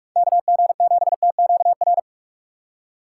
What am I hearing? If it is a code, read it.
Morse code, heard as a keyed tone, about 44 wpm, 700 Hz. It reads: KG7TQR
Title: USA Call Signs - 6 Characters - 3x Character Spacing - 45wpm